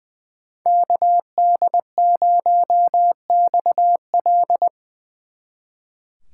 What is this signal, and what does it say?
Morse code, heard as a keyed tone, about 22 wpm, 700 Hz. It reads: KD0XL
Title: Speed Racing - USA 2x2 Call Signs 20wpm